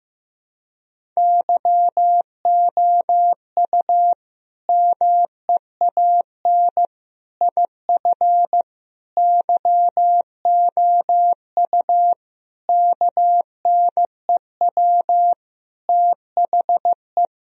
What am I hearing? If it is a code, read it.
Morse code, heard as a keyed tone, about 15 wpm, 700 Hz. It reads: YOU MEAN IF YOU KNEW THE
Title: Sentences from Top 1000 Words - Review Part 4 - 15wpm